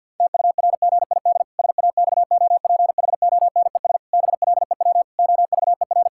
Morse code, heard as a keyed tone, about 50 wpm, 700 Hz. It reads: TURGID SAXOPHONES BLEW OVER
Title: Pangram Challenge v2 - 50wpm